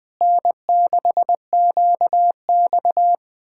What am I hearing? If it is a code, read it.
Morse code, heard as a keyed tone, about 20 wpm, 700 Hz. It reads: N6QX
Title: Speed-Racing Common QSO Phrases Encore - 17wpm